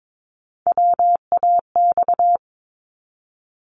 Morse code, heard as a keyed tone, about 22 wpm, 700 Hz. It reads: WAX
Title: Multi-letter Phonemes 22wpm